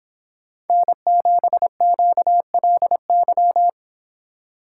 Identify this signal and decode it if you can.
Morse code, heard as a keyed tone, about 25 wpm, 700 Hz. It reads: N7QLY